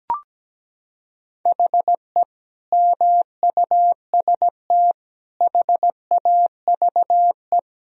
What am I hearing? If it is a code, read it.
Morse code, heard as a keyed tone, about 17 wpm, 700 Hz. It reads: HE MUST HAVE
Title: Sentences from Top 1000 Words - Review Part 1 - 17wpm